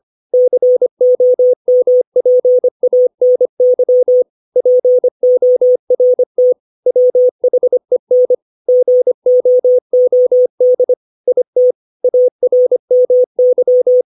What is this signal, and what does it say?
Morse code, heard as a keyed tone, about 25 wpm, 500 Hz. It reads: COMPANY PORT WHEN GOOD IT ARMY
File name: 632 words / 632 Words 25WPM